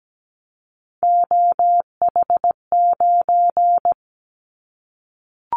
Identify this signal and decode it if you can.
Morse code, heard as a keyed tone, about 17 wpm, 700 Hz. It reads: OH9
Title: Callsign Prefixes for Top 50 DXCC - 17wpm